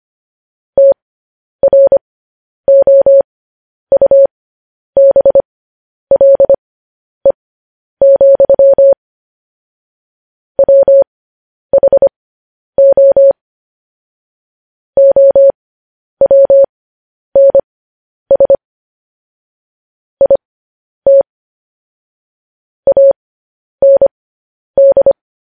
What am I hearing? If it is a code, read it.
Morse code, heard as a keyed tone, about 25 wpm, spaced as for 10 wpm, 550 Hz. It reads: TROUBLE, WHO OWNS IT AND
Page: news headlines in morse code at 10 wpm